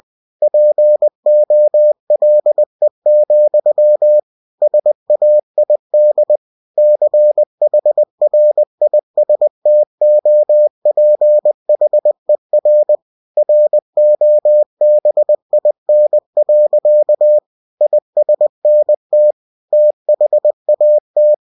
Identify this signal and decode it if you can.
Morse code, heard as a keyed tone, about 20 wpm, 600 Hz. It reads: POLE, SAID CHRISTOPHER ROBIN. ISNT THAT